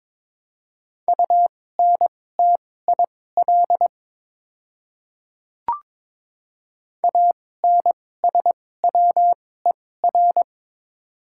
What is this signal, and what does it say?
Morse code, heard as a keyed tone, about 22 wpm, spaced as for 16 wpm, 700 Hz. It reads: UNTIL ANSWER
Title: 2x Character Spacing - Top 300 Words - 22wpm